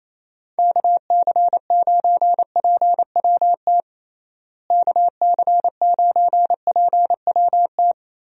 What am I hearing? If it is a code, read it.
Morse code, heard as a keyed tone, about 28 wpm, 700 Hz. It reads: KC9PWT KC9PWT
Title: USA Call Signs - 6 Characters - Repeated 2x - 28wpm